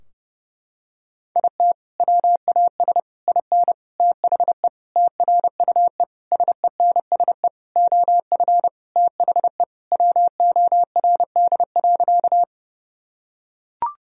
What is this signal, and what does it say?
Morse code, heard as a keyed tone, about 30 wpm, 700 Hz. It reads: IT WAS IN THE TRUE SENSE OF THE WORD.
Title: Sentences from Top 1000 Words - Review Part 4 - 30wpm